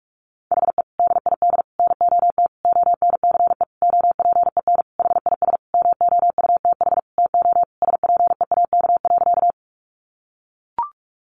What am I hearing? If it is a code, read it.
Morse code, heard as a keyed tone, about 44 wpm, 700 Hz. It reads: HE DID NOT ONCE OPEN HIS MOUTH TO SPEAK.
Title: Sentences from Top 1500 Words 45wpm